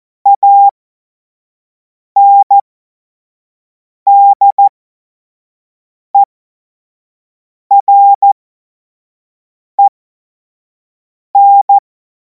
Morse code, heard as a keyed tone, about 14 wpm, spaced as for 5 wpm, 800 Hz. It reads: ANDEREN
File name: Morsekurs / Witze / Witz 09. Juni 2026 25 F70